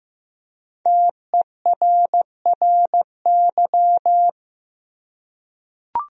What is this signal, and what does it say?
Morse code, heard as a keyed tone, about 16 wpm, 700 Hz. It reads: TERRY
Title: USA Common Names - 15wpm